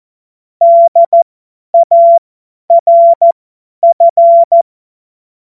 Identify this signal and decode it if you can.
Morse code, heard as a keyed tone, about 14 wpm, spaced as for 10 wpm, 680 Hz. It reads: DARF